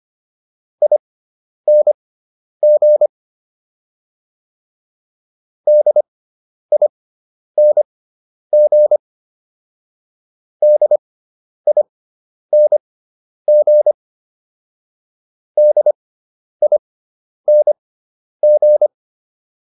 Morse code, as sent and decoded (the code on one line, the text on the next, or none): .. -. --. / -.. .. -. --. / -.. .. -. --. / -.. .. -. --.
ING DING DING DING